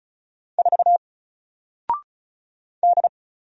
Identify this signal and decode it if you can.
Morse code, heard as a keyed tone, about 36 wpm, 700 Hz. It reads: 4 D
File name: Single-Letter-Number-35wpm